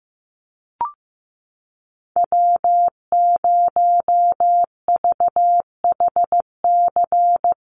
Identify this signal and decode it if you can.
Morse code, heard as a keyed tone, about 15 wpm, 700 Hz. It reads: W0VHC
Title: USA Call Signs Repeated 3x 15wpm